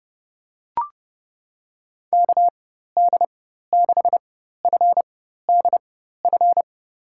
Morse code, heard as a keyed tone, about 32 wpm, spaced as for 14 wpm, 700 Hz. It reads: KD6FDF
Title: USA Call Signs - 6 Characters - 4x Character Spacing - 30wpm